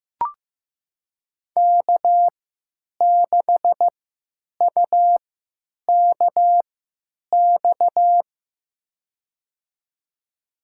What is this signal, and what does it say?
Morse code, heard as a keyed tone, about 16 wpm, spaced as for 9 wpm, 700 Hz. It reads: K6UKX
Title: USA Call Signs - 5 Characters - 3x Character Spacing - 15wpm